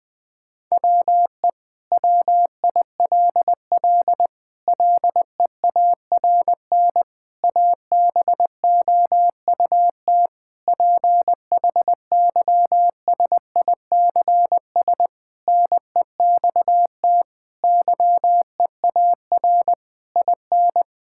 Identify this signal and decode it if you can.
Morse code, heard as a keyed tone, about 20 wpm, 700 Hz. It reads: WE WILL LEARN ABOUT PHYSICS NEXT YEAR IN
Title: Sentences from Top 1000 Words - Review Part 5 - 20wpm